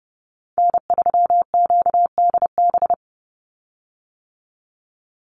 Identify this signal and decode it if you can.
Morse code, heard as a keyed tone, about 30 wpm, 700 Hz. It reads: N3QDB
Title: USA Call Signs 30wpm